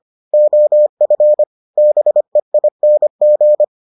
Morse code, heard as a keyed tone, about 25 wpm, 600 Hz. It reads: OF BEING